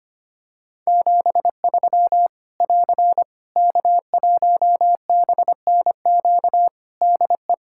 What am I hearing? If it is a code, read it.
Morse code, heard as a keyed tone, about 25 wpm, 700 Hz. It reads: 73 <AR> K1BNQ DE
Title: Practice QSO - Exchange 4 - Ending - 25wpm